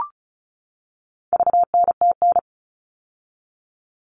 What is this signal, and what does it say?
Morse code, heard as a keyed tone, about 34 wpm, 700 Hz. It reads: VNTN
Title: All Cut Numbers - 4-Digit 35wpm